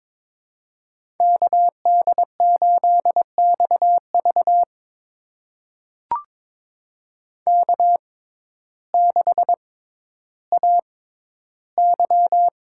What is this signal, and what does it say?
Morse code, heard as a keyed tone, about 22 wpm, spaced as for 12 wpm, 700 Hz. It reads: KD8XV K6AY
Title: USA Call Signs - 6x Character Spacing - 22wpm